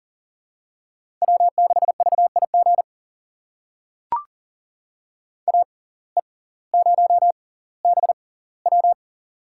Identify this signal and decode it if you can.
Morse code, heard as a keyed tone, about 39 wpm, spaced as for 20 wpm, 690 Hz. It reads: W6VIG AE0BW
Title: USA Call Signs - 5 Characters - 6x Character Spacing - 40wpm